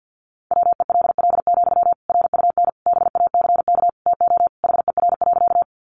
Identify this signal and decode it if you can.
Morse code, heard as a keyed tone, about 50 wpm, 700 Hz. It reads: WELL, RUN BACK TO HER.
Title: Sentences from Top 300 Words 50wpm